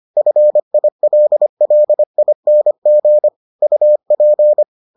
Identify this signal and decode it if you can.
Morse code, heard as a keyed tone, about 25 wpm, 600 Hz. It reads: FILLING UP